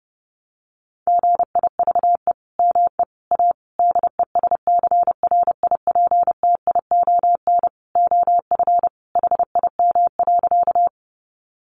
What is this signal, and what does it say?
Morse code, heard as a keyed tone, about 30 wpm, 700 Hz. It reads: GIVE ME A DESCRIPTION OF HIM.
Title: Sentences from Top 1300 Words 30wpm